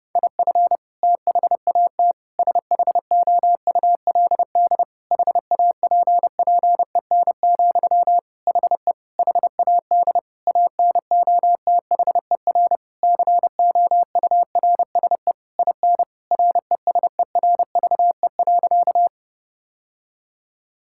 Morse code, heard as a keyed tone, about 30 wpm, 700 Hz. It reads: IF THAT SHOULD HAPPEN, HE HAD ANOTHER COURSE IN RESERVE.
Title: Sentences from Top 1000 Words - Review Part 6 - 30wpm